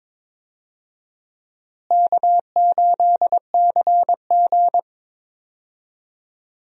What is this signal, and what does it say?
Morse code, heard as a keyed tone, about 22 wpm, 700 Hz. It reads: K8CG
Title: USA Call Signs - 4 Characters - Repeated 3x - 22wpm